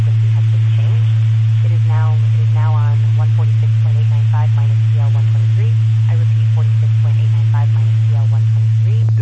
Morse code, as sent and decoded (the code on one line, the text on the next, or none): none